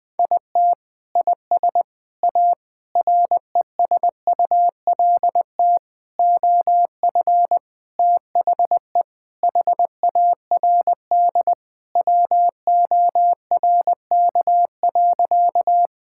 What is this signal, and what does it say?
Morse code, heard as a keyed tone, about 20 wpm, 700 Hz. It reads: IT IS A RESULT OF THE HARD WORK.
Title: Sentences with 4-Word Phrases - Spoken Beginning - 20wpm